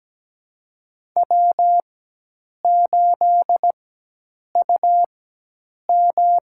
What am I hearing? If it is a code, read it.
Morse code, heard as a keyed tone, about 17 wpm, spaced as for 8 wpm, 700 Hz. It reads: W8UM
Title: USA Call Signs - 4 Characters - 4x Character Spacing - 17wpm